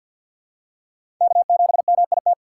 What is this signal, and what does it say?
Morse code, heard as a keyed tone, about 53 wpm, 700 Hz. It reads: K7MIT
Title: USA Call Signs - 5 Characters - 6x Character Spacing - 50wpm